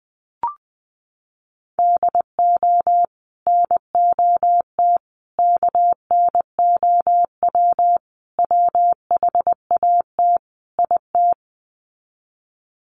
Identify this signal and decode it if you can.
Morse code, heard as a keyed tone, about 20 wpm, 700 Hz. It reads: DO NOT KNOW WHAT IT
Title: Sets of 5 Words 20wpm